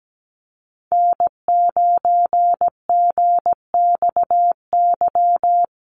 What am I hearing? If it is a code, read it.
Morse code, heard as a keyed tone, about 17 wpm, 700 Hz. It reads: N9GXY
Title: USA Call Signs Repeated 3x 17wpm